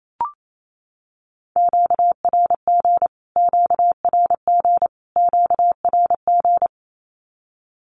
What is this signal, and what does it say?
Morse code, heard as a keyed tone, about 28 wpm, 700 Hz. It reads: QRG QRG QRG
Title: Q-Codes Repeated 3x 28wpm